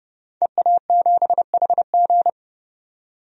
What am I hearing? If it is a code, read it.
Morse code, heard as a keyed tone, about 32 wpm, 700 Hz. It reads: EA7HG
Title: DXCC-100 Call Signs - 5 Characters - 30wpm